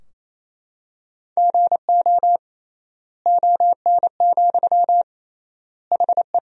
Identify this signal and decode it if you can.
Morse code, heard as a keyed tone, about 28 wpm, 700 Hz. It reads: GO ON, HE
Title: 3x Word-Spacing - Sentences from Top 100 Words - 28wpm